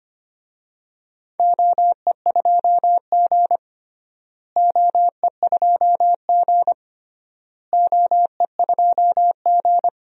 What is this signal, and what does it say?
Morse code, heard as a keyed tone, about 25 wpm, 700 Hz. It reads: OE2G OE2G OE2G
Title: DXCC-100 Call Signs - 4 Characters - Repeated 3x - 25wpm